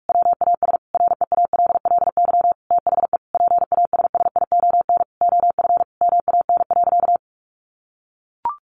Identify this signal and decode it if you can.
Morse code, heard as a keyed tone, about 44 wpm, 700 Hz. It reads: WAS REALLY THE PASSION OF MAN.
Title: Sentences from Top 1400 Words 45wpm